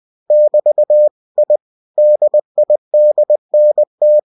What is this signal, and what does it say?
Morse code, heard as a keyed tone, about 20 wpm, 600 Hz. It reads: <BT> I DIDNT
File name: cw-books / winnie-the-pooh / winnie-20wpm-page-0079